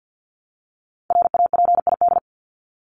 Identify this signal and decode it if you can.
Morse code, heard as a keyed tone, about 50 wpm, 700 Hz. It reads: RAPID
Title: LY Suffix - Top 100 Words with Base Words 50wpm